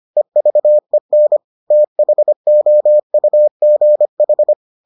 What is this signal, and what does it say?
Morse code, heard as a keyed tone, about 25 wpm, 600 Hz. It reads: EVEN THOUGH